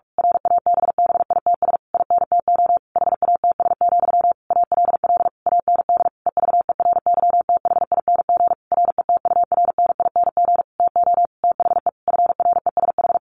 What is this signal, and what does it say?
Morse code, heard as a keyed tone, about 44 wpm, 700 Hz. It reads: RABBITS INTO HATS, ALL AND EVERYTHING RETURNING TO THE FRESH
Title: A Sound Of Thunder 45wpm